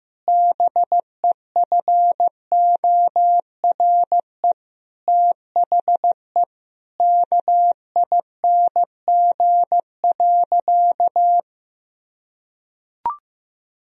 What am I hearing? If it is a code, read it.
Morse code, heard as a keyed tone, about 14 wpm, 700 Hz. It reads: BEFORE THE KING.
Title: Sentences from Top 1000 Words - Review Part 2 - 15wpm